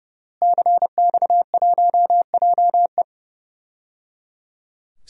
Morse code, heard as a keyed tone, about 29 wpm, 700 Hz. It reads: CX1JE
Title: DXCC-100 Call Signs - 5 Characters - Repeated 3x - 30wpm